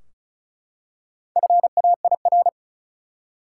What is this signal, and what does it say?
Morse code, heard as a keyed tone, about 34 wpm, 700 Hz. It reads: FAIR
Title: Four Letter Words - 35wpm